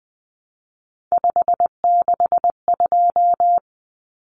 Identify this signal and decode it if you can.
Morse code, heard as a keyed tone, about 20 wpm, 700 Hz. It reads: 562
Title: Common Cut Numbers - 3-Digit 20wpm